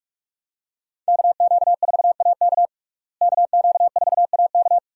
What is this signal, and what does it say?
Morse code, heard as a keyed tone, about 47 wpm, 700 Hz. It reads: KQ4AK KQ4AK
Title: USA Call Signs - 5 Characters - Repeated 2x - 45wpm